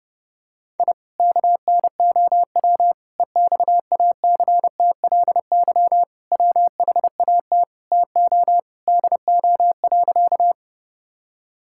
Morse code, heard as a keyed tone, about 30 wpm, 700 Hz. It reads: I KNOW EXACTLY WHAT TO DO.
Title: Sentences from Top 1000 Words - Review Part 6 - 30wpm